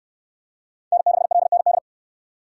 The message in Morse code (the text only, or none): N6LND